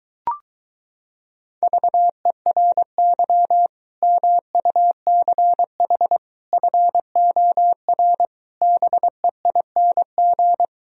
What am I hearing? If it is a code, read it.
Morse code, heard as a keyed tone, about 22 wpm, 700 Hz. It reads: VERY MUCH FOR BEING